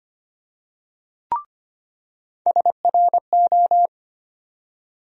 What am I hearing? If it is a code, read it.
Morse code, heard as a keyed tone, about 25 wpm, 700 Hz. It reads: SRO
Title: USA Call Sign Postfixes - 3 Letters 25wpm